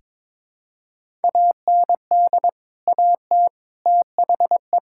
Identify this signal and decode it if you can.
Morse code, heard as a keyed tone, about 22 wpm, 700 Hz. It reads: AND AT THE
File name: practice / 22wpm / Build A Phrase - 5-Word Phrases - 22wpm